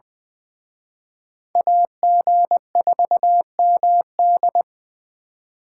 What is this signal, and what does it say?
Morse code, heard as a keyed tone, about 24 wpm, 700 Hz. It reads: AG4MD
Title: Speed Racing - USA 2x2 Call Signs 20wpm